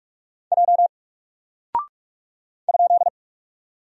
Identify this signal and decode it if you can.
Morse code, heard as a keyed tone, about 44 wpm, 700 Hz. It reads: J ?
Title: Single Character (Letter:Number:Punctuation:Prosign) 45wpm